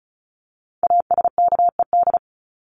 Morse code, heard as a keyed tone, about 34 wpm, 700 Hz. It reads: ASKED
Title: ED Suffix - Top 100 Words with Base Words 35wpm